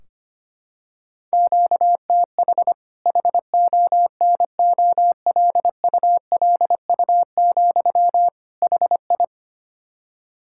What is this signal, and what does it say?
Morse code, heard as a keyed tone, about 25 wpm, 700 Hz. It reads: QTH HONOLULU, HI
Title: K1USN SST QSO Practice 25wpm